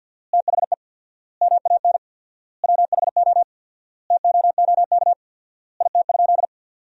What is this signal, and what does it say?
Morse code, heard as a keyed tone, about 52 wpm, 700 Hz. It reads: THE MAN WHO TOOK IT?
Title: 4x Word-Spacing - Sentences from Top 300 Words - 50wpm